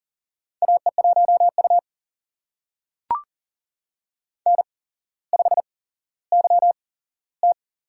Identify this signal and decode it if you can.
Morse code, heard as a keyed tone, about 40 wpm, spaced as for 17 wpm, 700 Hz. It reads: AE1U N5YT
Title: USA Call Signs - 4 Characters - 8x Character Spacing - 40wpm